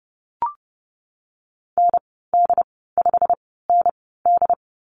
Morse code, heard as a keyed tone, about 32 wpm, 700 Hz. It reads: N D 5 N D
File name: Lesson 19g - Letter w - Call Signs 30wpm